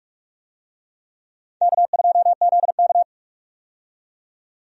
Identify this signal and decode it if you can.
Morse code, heard as a keyed tone, about 45 wpm, 700 Hz. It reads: K2ZK
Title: USA Call Signs - 4 Characters - 45wpm